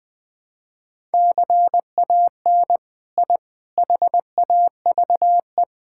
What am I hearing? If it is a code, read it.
Morse code, heard as a keyed tone, about 21 wpm, 700 Hz. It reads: CAN I HAVE
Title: All-Age Appropriate Jokes - 20wpm